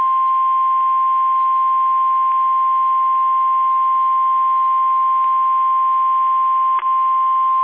Emergency Alert System (SAME) tones (the 1050 Hz attention tone). It is heard from 1.1 to 7.2 s.